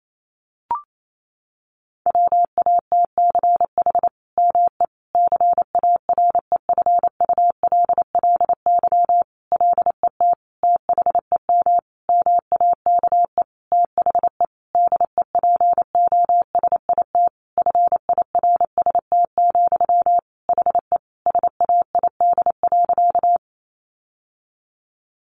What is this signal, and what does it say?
Morse code, heard as a keyed tone, about 28 wpm, 700 Hz. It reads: WATCH ME CAREFULLY LET THEM MAKE THE DEPOSIT FIRST, HE SAID.
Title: Sentences from Top 1300 Words 28wpm